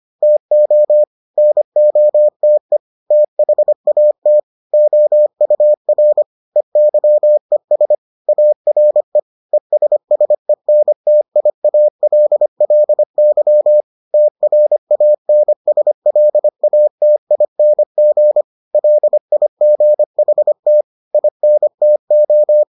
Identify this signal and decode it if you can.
Morse code, heard as a keyed tone, about 25 wpm, 600 Hz. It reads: TO NOTE THAT OUR EYES ARE ESSENTIALLY TRANSLATING LIGHT INTO